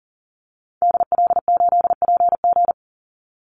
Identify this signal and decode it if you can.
Morse code, heard as a keyed tone, about 40 wpm, 700 Hz. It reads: DL8PG
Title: Call Signs Repeated 2x - 40wpm